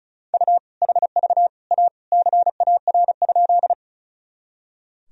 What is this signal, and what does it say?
Morse code, heard as a keyed tone, about 34 wpm, 700 Hz. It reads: U HV A CAR?
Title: Sentences from Top 300 Words and Abbreviations 35wpm